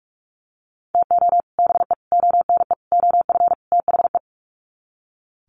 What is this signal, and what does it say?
Morse code, heard as a keyed tone, about 44 wpm, 700 Hz. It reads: TO BE ONE OF THE
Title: Sets of 5 Words 45wpm